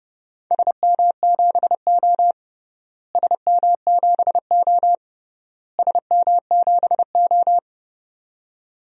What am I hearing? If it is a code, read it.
Morse code, heard as a keyed tone, about 30 wpm, 700 Hz. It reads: SM7O SM7O SM7O
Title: DXCC-100 Call Signs - 4 Characters - Repeated 3x - 30wpm